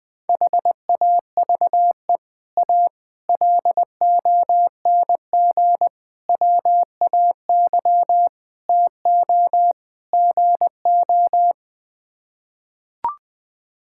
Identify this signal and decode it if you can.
Morse code, heard as a keyed tone, about 20 wpm, 700 Hz. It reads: HAVE A LONG WAY TO GO
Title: Sets of 6 Words - 20wpm